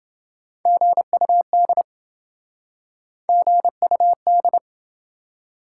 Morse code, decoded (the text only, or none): GUD GUD